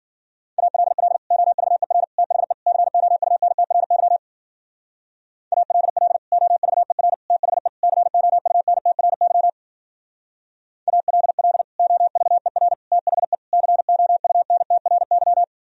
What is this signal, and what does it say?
Morse code, heard as a keyed tone, about 48 wpm, 700 Hz. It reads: ALL OVER THE COUNTRY ALL OVER THE COUNTRY ALL OVER THE COUNTRY
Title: Speed-Racing - Sets of 4 Words 40wpm